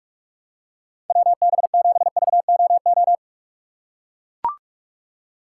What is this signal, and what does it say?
Morse code, heard as a keyed tone, about 42 wpm, 700 Hz. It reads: WB7VOO
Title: USA Call Signs - 6 Characters - 3x Character Spacing - 45wpm